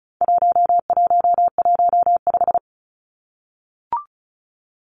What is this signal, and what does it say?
Morse code, heard as a keyed tone, about 34 wpm, 700 Hz. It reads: 1115
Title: Numbers - 4-Digit 35wpm